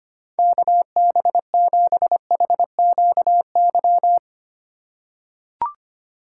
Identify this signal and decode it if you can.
Morse code, heard as a keyed tone, about 25 wpm, 690 Hz. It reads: KB7HQY
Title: USA Call Signs - 6 Characters - 2x Character Spacing - 25wpm